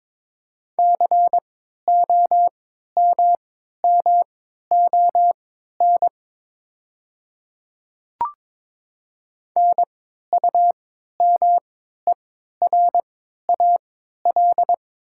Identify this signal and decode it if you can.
Morse code, heard as a keyed tone, about 22 wpm, spaced as for 13 wpm, 700 Hz. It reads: COMMON NUMERAL
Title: 3x Character Spacing - Top 500 Words - 22wpm